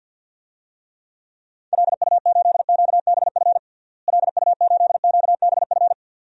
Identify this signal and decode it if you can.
Morse code, heard as a keyed tone, about 49 wpm, 700 Hz. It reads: LU8QBF LU8QBF
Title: DXCC-100 Call Signs - 6 Characters - Repeated 2x - 50wpm